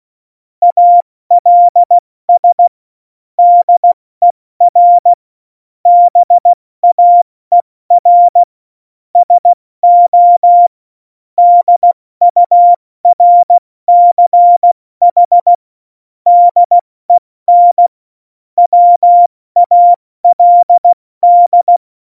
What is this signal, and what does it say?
Morse code, heard as a keyed tone, about 16 wpm, 700 Hz. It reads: ALS DER BAER SO DURCH DEN WALD
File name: Morsekurs / Witze / Witz 17.Juli 2018, 70 F80